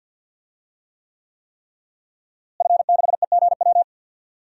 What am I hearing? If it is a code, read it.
Morse code, heard as a keyed tone, about 48 wpm, 700 Hz. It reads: F6EGW
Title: DXCC-100 Call Signs - 5 Characters - Repeated 3x - 50wpm